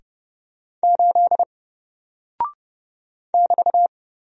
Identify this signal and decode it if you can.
Morse code, heard as a keyed tone, about 30 wpm, 700 Hz. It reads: 8 <BT>
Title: Single Character (Letter:Number:Punctuation:Prosign) Alt 30wpm